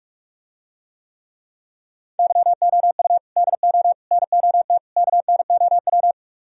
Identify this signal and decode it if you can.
Morse code, heard as a keyed tone, about 49 wpm, 700 Hz. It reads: YOU DO NOT KNOW